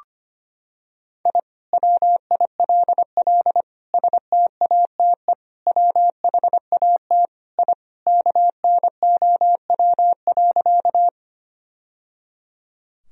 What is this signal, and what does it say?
Morse code, heard as a keyed tone, about 25 wpm, 700 Hz. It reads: I WILL STATE WHAT I KNOW.
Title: Sentences from Top 1000 Words - Review Part 2 - 25wpm